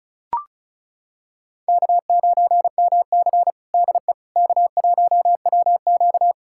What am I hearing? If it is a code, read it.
Morse code, heard as a keyed tone, about 36 wpm, 700 Hz. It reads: K9MC DE K1WQ
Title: USA Callsign DE Callsign - 4 Characters - 35wpm